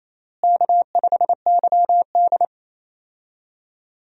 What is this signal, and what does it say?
Morse code, heard as a keyed tone, about 28 wpm, 700 Hz. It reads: K5YD
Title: USA Call Signs - 4 Characters - Repeated 3x - 28wpm